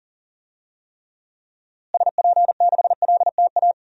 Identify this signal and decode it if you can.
Morse code, heard as a keyed tone, about 40 wpm, 700 Hz. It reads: SP6LTA
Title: DXCC-100 Call Signs - 6 Characters - Repeated 3x - 40wpm